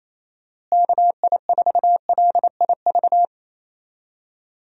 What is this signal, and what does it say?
Morse code, heard as a keyed tone, about 28 wpm, 700 Hz. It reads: KI4LIV
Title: USA Call Signs - 6 Characters - Repeated 3x - 28wpm